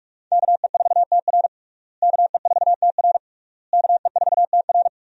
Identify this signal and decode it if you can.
Morse code, heard as a keyed tone, about 46 wpm, 700 Hz. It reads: KE4TR KE4TR KE4TR
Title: USA Call Signs - 5 Characters - Repeated 3x - 45wpm